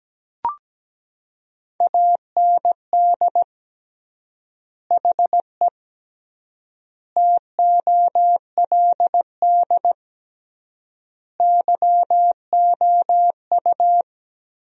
Morse code, heard as a keyed tone, about 17 wpm, 700 Hz. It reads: AND HE TOLD YOU
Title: 3x Word-Spacing - Sentences from Top 500 Words - 17wpm